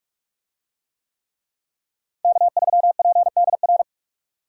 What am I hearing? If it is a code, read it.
Morse code, heard as a keyed tone, about 40 wpm, 700 Hz. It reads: K3PDR